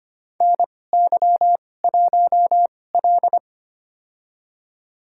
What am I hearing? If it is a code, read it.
Morse code, heard as a keyed tone, about 25 wpm, spaced as for 18 wpm, 700 Hz. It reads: NY1L
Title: USA Call Signs - 4 Characters - 2x Character Spacing - 25wpm